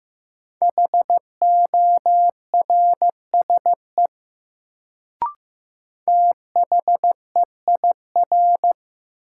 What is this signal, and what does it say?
Morse code, heard as a keyed tone, about 15 wpm, 700 Hz. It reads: HORSE THEIR
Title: Sentences from Top 300 Words - New Word Drill - 15wpm